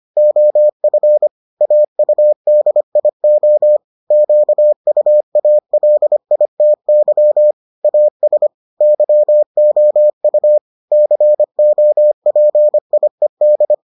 Morse code, heard as a keyed tone, about 25 wpm, 600 Hz. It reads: OF AUDIO QUALITY AS YOU COPIED